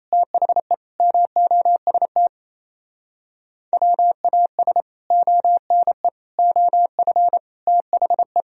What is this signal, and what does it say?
Morse code, heard as a keyed tone, about 37 wpm, 700 Hz. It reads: THE MOST WAS ONE OF THE